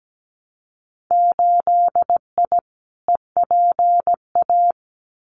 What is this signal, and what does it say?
Morse code, heard as a keyed tone, about 17 wpm, 700 Hz. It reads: 8I EPA
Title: Winter Field Day Exchanges Encore 17wpm